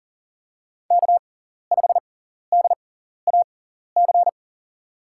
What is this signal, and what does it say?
Morse code, heard as a keyed tone, about 40 wpm, spaced as for 14 wpm, 700 Hz. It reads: K5DAC
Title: USA Call Signs - 6x Character Spacing - 40wpm